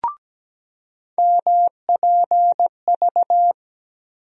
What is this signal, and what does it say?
Morse code, heard as a keyed tone, about 17 wpm, 700 Hz. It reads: MPV
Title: USA Call Sign Postfixes - 3 Letters 17wpm